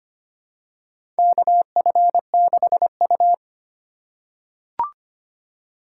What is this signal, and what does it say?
Morse code, heard as a keyed tone, about 25 wpm, 700 Hz. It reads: KF6U
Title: USA Call Signs - 4 Characters - 3x Character Spacing - 25wpm